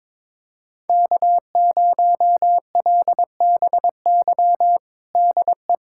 Morse code, heard as a keyed tone, about 22 wpm, 700 Hz. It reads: K0LBY DE
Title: USA Callsign DE Callsign - 5 Characters - 22wpm